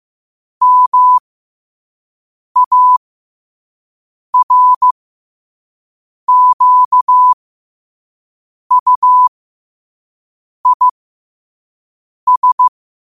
Morse code, heard as a keyed tone, about 15 wpm, spaced as for 5 wpm, 1000 Hz. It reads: MARQUIS
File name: 5wpm